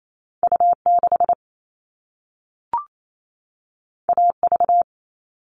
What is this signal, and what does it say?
Morse code, heard as a keyed tone, about 28 wpm, 700 Hz. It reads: U6 AV
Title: All Cut Numbers - 2-Digit 28wpm